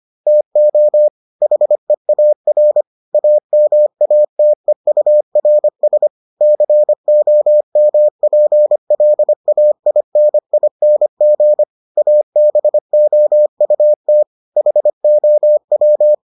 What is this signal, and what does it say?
Morse code, heard as a keyed tone, about 25 wpm, 600 Hz. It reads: TO HEAR AMATEURS COMPLAINING ABOUT HOW